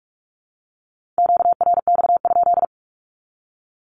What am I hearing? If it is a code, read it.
Morse code, heard as a keyed tone, about 44 wpm, 700 Hz. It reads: QRX?
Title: Q-Codes 45wpm